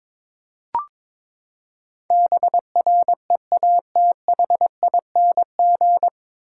Morse code, heard as a keyed tone, about 22 wpm, 700 Hz. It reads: BREATHING
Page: Morse Code Ninja practice 22wpm